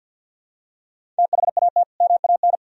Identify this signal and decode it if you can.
Morse code, heard as a keyed tone, about 51 wpm, 700 Hz. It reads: THAT MAN
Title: Sentences from Top 1000 Words - Review Part 1 - 50wpm